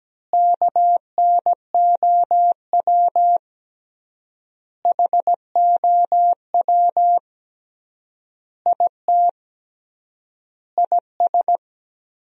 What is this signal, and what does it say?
Morse code, heard as a keyed tone, about 17 wpm, 700 Hz. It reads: KNOW HOW IT IS
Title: 3x Word-Spacing - Sentences from Top 100 Words - 17wpm